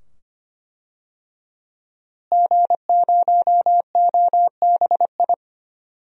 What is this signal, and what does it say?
Morse code, heard as a keyed tone, about 25 wpm, 700 Hz. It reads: G0OBI
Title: DXCC-100 Call Signs - 5 Characters - Repeated 2x - 25wpm